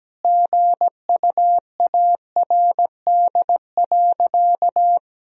Morse code, heard as a keyed tone, about 17 wpm, 700 Hz. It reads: GUARD.
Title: Sentences from Top 1000 Words - Review Part 4 - 17wpm